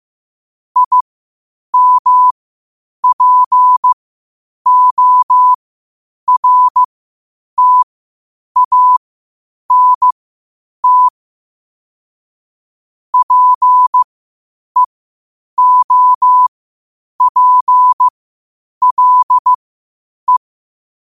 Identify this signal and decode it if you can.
Morse code, heard as a keyed tone, about 15 wpm, spaced as for 8 wpm, 1000 Hz. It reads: IMPORTANT PEOPLE
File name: qotdfiles / 8wpm